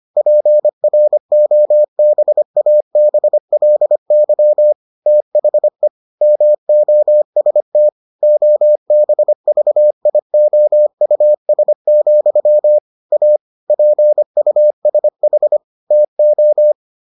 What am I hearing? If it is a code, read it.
Morse code, heard as a keyed tone, about 25 wpm, 600 Hz. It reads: PROBABLY THE MOST OBVIOUS, A PUSH TO